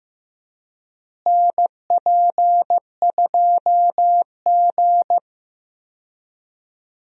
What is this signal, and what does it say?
Morse code, heard as a keyed tone, about 15 wpm, 700 Hz. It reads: NP2G
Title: USA Call Signs - 4 Characters - Repeated 3x - 15wpm